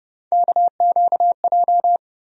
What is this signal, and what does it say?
Morse code, heard as a keyed tone, about 30 wpm, 700 Hz. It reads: KQJ
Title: USA Call Sign Postfixes - 3 Letters 30wpm